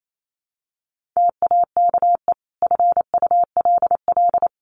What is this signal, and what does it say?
Morse code, heard as a keyed tone, about 28 wpm, 700 Hz. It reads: TAKE FULL
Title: Sentences from Top 1400 Words 28wpm